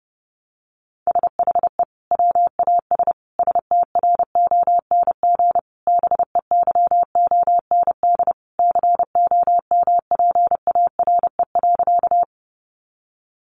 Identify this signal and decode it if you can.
Morse code, heard as a keyed tone, about 28 wpm, 700 Hz. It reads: SHE WAS STRONG BEYOND COMPARE.
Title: Sentences from Top 1300 Words 30wpm